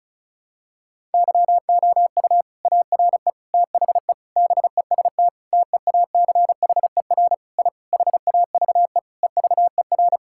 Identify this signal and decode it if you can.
Morse code, heard as a keyed tone, about 36 wpm, 700 Hz. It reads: YOU ARE THE BEST TEACHER I HAVE EVER
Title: Sentences from Top 1000 Words - Review Part 6 - 35wpm